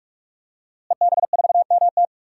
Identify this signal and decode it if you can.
Morse code, heard as a keyed tone, about 46 wpm, 700 Hz. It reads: EB4MT
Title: DXCC-100 Call Signs - 5 Characters - Repeated 3x - 45wpm